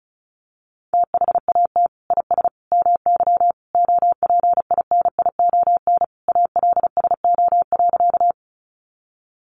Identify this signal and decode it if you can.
Morse code, heard as a keyed tone, about 34 wpm, 700 Hz. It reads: THAT IS MY OPINION ALSO.
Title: Sentences from Top 1200 Words 35wpm